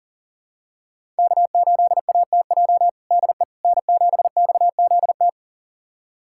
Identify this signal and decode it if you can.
Morse code, heard as a keyed tone, about 40 wpm, 700 Hz. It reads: K8ATJ DE N7XZT
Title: USA Callsign DE Callsign - 5 Characters - 40wpm